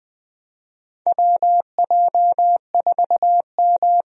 Morse code, heard as a keyed tone, about 20 wpm, 700 Hz. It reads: WJ4M
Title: USA Call Signs - 4 Characters - Repeated 3x - 20wpm